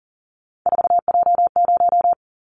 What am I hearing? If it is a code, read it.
Morse code, heard as a keyed tone, about 40 wpm, 700 Hz. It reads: 4J0
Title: Callsign Prefixes for Top 100 DXCC - 40wpm